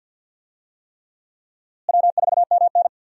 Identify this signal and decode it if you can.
Morse code, heard as a keyed tone, about 51 wpm, 700 Hz. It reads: W4MN
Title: USA Call Signs - 4 Characters - Repeated 3x - 50wpm